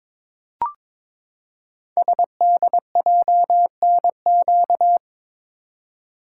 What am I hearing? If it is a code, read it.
Morse code, heard as a keyed tone, about 22 wpm, 700 Hz. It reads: SDJNQ
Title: Code Groups - 5 Letter:Numbers - 22wpm